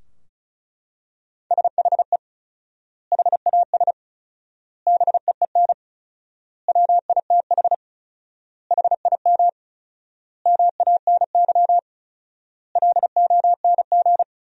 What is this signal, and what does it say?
Morse code, heard as a keyed tone, about 34 wpm, 700 Hz. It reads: SHE HAS BEEN WITH HIM MANY LONG
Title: 4x Word-Spacing - Sentences from Top 200 Words - 35wpm